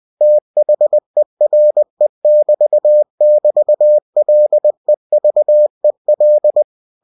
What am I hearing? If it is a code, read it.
Morse code, heard as a keyed tone, about 20 wpm, 600 Hz. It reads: THERE<BT><BT>LEVEL
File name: cw-books / winnie-the-pooh / winnie-20wpm-page-0006